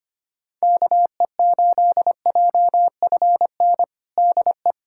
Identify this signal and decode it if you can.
Morse code, heard as a keyed tone, about 25 wpm, 700 Hz. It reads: KE8JFN DE
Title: USA Callsign DE Callsign - 6 Characters - 25wpm